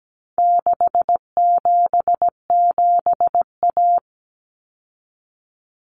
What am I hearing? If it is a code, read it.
Morse code, heard as a keyed tone, about 17 wpm, 700 Hz. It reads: 677A
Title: Common Cut Numbers - 4-Digit 17wpm